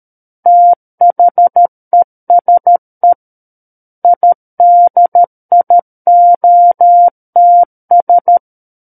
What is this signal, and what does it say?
Morse code, heard as a keyed tone, about 13 wpm, 700 Hz. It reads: THESE IDIOTS
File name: Book One / 04-Chapter 4 - 13 WPM